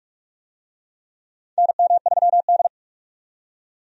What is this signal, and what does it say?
Morse code, heard as a keyed tone, about 44 wpm, 700 Hz. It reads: NM3D
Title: USA Call Signs - 4 Characters - 45wpm